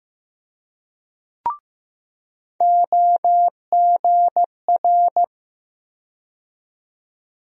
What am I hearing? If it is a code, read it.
Morse code, heard as a keyed tone, about 15 wpm, 700 Hz. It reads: OGR
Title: USA Call Sign Postfixes - 3 Letters 15wpm